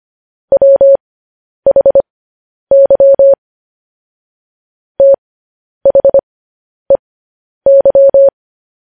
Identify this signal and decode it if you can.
Morse code, heard as a keyed tone, about 25 wpm, spaced as for 10 wpm, 550 Hz. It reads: WHY THEY
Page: News Headlines in Morse Code at 10 WPM